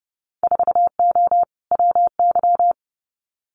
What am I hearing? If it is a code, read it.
Morse code, heard as a keyed tone, about 30 wpm, 700 Hz. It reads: 4O WY
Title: Winter Field Day Exchanges Encore 30wpm